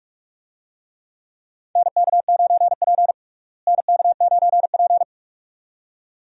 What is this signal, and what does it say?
Morse code, heard as a keyed tone, about 41 wpm, 700 Hz. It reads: NK9P NK9P
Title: USA Call Signs - 4 Characters - Repeated 2x - 45wpm